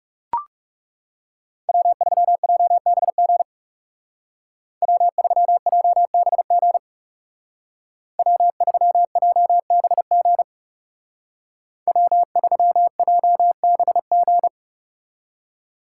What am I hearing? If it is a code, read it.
Morse code, heard as a keyed tone, about 40 wpm, 700 Hz. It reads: W3JBG W3JBG W3JBG W3JBG